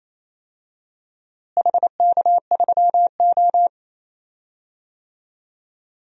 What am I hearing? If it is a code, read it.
Morse code, heard as a keyed tone, about 28 wpm, 700 Hz. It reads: HK3O
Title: DXCC-100 Call Signs - 4 Characters - Repeated 2x - 28wpm